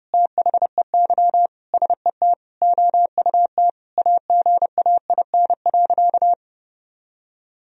Morse code, heard as a keyed tone, about 30 wpm, 700 Hz. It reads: THEY SET OUT AGAIN.
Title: Sentences from Top 1000 Words - Review Part 4 - 30wpm